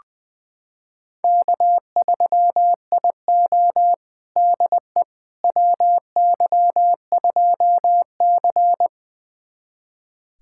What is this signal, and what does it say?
Morse code, heard as a keyed tone, about 20 wpm, 700 Hz. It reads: K3IO DE WY2C
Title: USA Callsign DE Callsign - 4 Characters - 20wpm